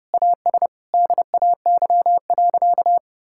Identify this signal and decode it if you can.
Morse code, heard as a keyed tone, about 30 wpm, 700 Hz. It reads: AS DAY.
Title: Sentences from Top 1000 Words - Review Part 5 - 30wpm